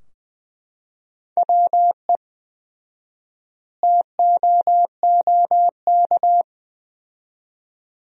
Morse code, heard as a keyed tone, about 20 wpm, 700 Hz. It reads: WE TOOK
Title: 4x Word-Spacing - Sentences from Top 400 Words - 20wpm